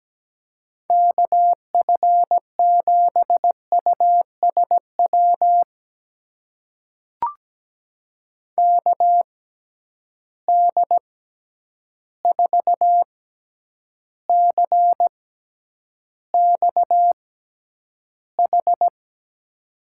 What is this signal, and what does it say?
Morse code, heard as a keyed tone, about 17 wpm, spaced as for 9 wpm, 700 Hz. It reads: KF7USW KD4CXH
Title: USA Call Signs - 6 Characters - 6x Character Spacing - 17wpm